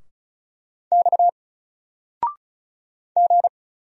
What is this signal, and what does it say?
Morse code, heard as a keyed tone, about 36 wpm, 700 Hz. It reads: X G
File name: Single-Letter-Number-35wpm